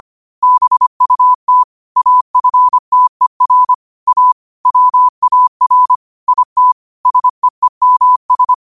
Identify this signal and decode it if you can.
Morse code, heard as a keyed tone, about 25 wpm, 1000 Hz. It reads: BUT AFTER A WAR IT SEEMS
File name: qotdfiles / 25wpm